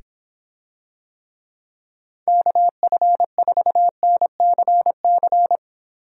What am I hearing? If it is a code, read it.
Morse code, heard as a keyed tone, about 26 wpm, 700 Hz. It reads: KF4NCC